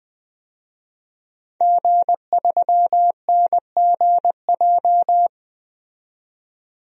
Morse code, heard as a keyed tone, about 20 wpm, 700 Hz. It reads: G3NGJ
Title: DXCC-100 Call Signs - 5 Characters - 20wpm